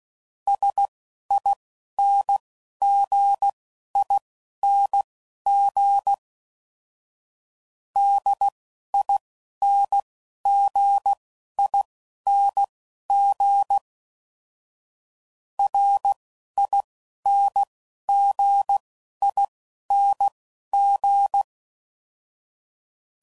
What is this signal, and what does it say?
Morse code, heard as a keyed tone, about 16 wpm, spaced as for 11 wpm, 800 Hz. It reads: SINGING DINGING RINGING
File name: Suffixes / Special INGs / Dble INGs